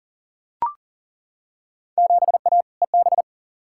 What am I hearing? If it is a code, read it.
Morse code, heard as a keyed tone, about 40 wpm, 700 Hz. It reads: 7A EB
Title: Field Day Exchanges Encore - 40wpm